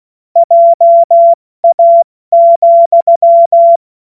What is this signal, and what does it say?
Morse code, heard as a keyed tone, about 16 wpm, spaced as for 14 wpm, 680 Hz. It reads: JA,
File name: Morsekurs / Witze / Witz 02. Mai 2023 70 F80